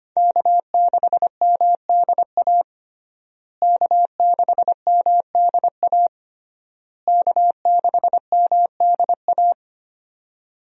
Morse code, heard as a keyed tone, about 25 wpm, 700 Hz. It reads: K6MDA K6MDA K6MDA
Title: USA Call Signs - 5 Characters - Repeated 3x - 25wpm